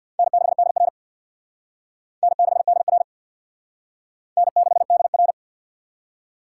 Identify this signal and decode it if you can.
Morse code, heard as a keyed tone, about 55 wpm, 700 Hz. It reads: N6DR N6DR N6DR
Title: Speed Racing - USA 1x2 Call Signs 50wpm